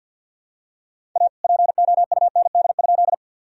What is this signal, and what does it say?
Morse code, heard as a keyed tone, about 50 wpm, 700 Hz. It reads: A POUND?
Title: Sentences from Top 1000 Words - Review Part 2 - 50wpm